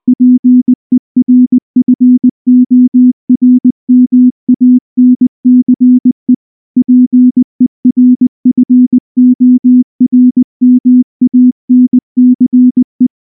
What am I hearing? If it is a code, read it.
Morse code, heard as a keyed tone, about 19 wpm, 250 Hz. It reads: PERFORMANCE PERFORMANCE
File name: morse / morse-training-20WPM-7